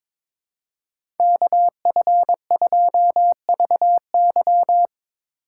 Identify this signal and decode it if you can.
Morse code, heard as a keyed tone, about 22 wpm, 700 Hz. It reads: KF2VY
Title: USA Call Signs - 5 Characters - 2x Character Spacing - 22wpm